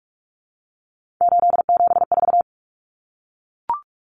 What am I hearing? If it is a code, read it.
Morse code, heard as a keyed tone, about 44 wpm, 700 Hz. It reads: 874
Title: Numbers - 3-Digit 45wpm